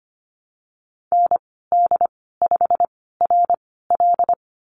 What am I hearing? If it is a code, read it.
Morse code, heard as a keyed tone, about 25 wpm, spaced as for 16 wpm, 700 Hz. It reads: ND5RL
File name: Lesson 15g - Number 5 - Call Signs 25wpm